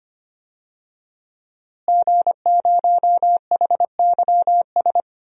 Morse code, heard as a keyed tone, about 25 wpm, 700 Hz. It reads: G0HYS
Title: DXCC-100 Call Signs - 5 Characters - Repeated 3x - 25wpm